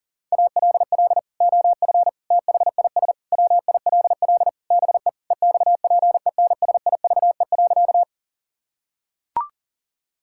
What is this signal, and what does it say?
Morse code, heard as a keyed tone, about 40 wpm, 700 Hz. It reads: ALL OF THIS WILL BE EXPENSIVE.
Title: Sentences from Top 1000 Words - Review Part 6 - 40wpm